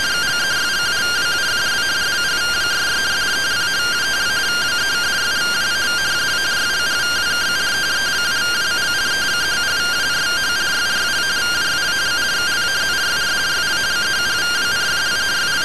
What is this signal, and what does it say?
Two-tone teleprinter signal, about 170 Hz shift.